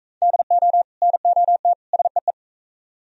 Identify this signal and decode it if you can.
Morse code, heard as a keyed tone, about 43 wpm, 700 Hz. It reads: DO NOT SEE